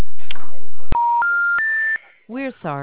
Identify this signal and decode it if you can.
Special information tone, with speech.